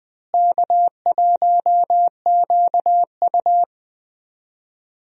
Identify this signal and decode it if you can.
Morse code, heard as a keyed tone, about 20 wpm, 700 Hz. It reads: K1QU
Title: USA Call Signs - 4 Characters - Repeated 3x - 20wpm